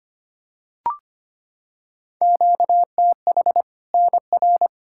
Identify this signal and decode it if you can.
Morse code, heard as a keyed tone, about 25 wpm, 700 Hz. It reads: QTH NR
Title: K1USN SST QSO Practice 25wpm